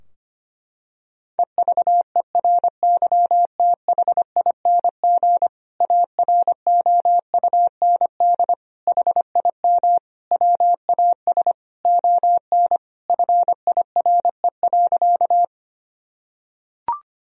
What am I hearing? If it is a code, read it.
Morse code, heard as a keyed tone, about 25 wpm, 700 Hz. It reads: EVERYTHING AROUND HIM WAS ON FIRE.
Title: Sentences from Top 1000 Words - Review Part 3 - 25wpm